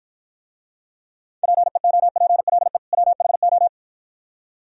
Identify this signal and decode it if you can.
Morse code, heard as a keyed tone, about 51 wpm, 700 Hz. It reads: PEOPLE WHO